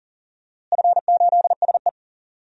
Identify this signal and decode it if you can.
Morse code, heard as a keyed tone, about 40 wpm, 700 Hz. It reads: F8SE
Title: DXCC-100 Call Signs - 4 Characters - Repeated 2x - 40wpm